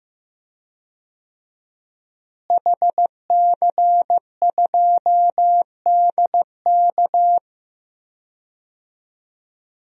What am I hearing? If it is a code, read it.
Morse code, heard as a keyed tone, about 15 wpm, 700 Hz. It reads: HC2DK
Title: DXCC-100 Call Signs - 5 Characters - Repeated 3x - 15wpm